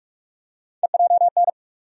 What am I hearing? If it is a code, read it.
Morse code, heard as a keyed tone, about 42 wpm, 700 Hz. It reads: EJN
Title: USA Call Sign Postfixes - 3 Letters 45wpm